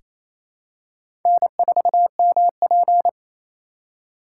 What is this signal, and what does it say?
Morse code, heard as a keyed tone, about 28 wpm, 700 Hz. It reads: N4MP
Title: USA Call Signs - 4 Characters - 3x Character Spacing - 28wpm